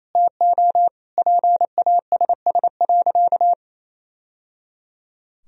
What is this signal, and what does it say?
Morse code, heard as a keyed tone, about 27 wpm, 700 Hz. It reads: TO PASS.
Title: Sentences from Top 1000 Words - Review Part 2 - 28wpm